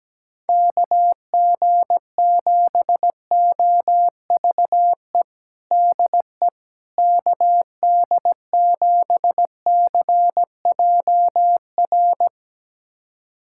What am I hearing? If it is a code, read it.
Morse code, heard as a keyed tone, about 17 wpm, 700 Hz. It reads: KG7OVE DE KD7CJR
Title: USA Callsign DE Callsign 17wpm